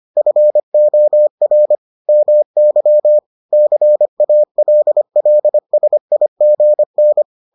Morse code, heard as a keyed tone, about 25 wpm, 600 Hz. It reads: FOR MY CALLSIGN